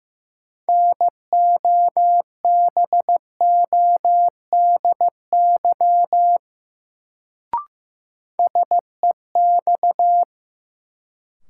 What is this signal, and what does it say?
Morse code, heard as a keyed tone, about 15 wpm, 700 Hz. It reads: NOBODY SEX
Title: Sentences from Top 1600 Words - New Word Drill Alt - 15wpm